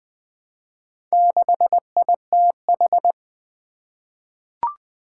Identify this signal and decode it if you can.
Morse code, heard as a keyed tone, about 20 wpm, 700 Hz. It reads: 6ITH
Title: Four Challenging Letter-Number Combinations 20wpm